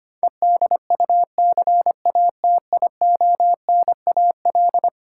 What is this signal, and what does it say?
Morse code, heard as a keyed tone, about 25 wpm, 700 Hz. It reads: EDUCATIONAL
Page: morse-code-ninja-practice-25wpm